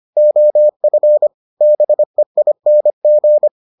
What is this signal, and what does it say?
Morse code, heard as a keyed tone, about 25 wpm, 600 Hz. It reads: OF BEING